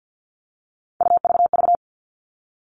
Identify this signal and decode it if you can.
Morse code, heard as a keyed tone, about 50 wpm, 700 Hz. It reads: UVV
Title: All Cut Numbers - 3-Digit 50wpm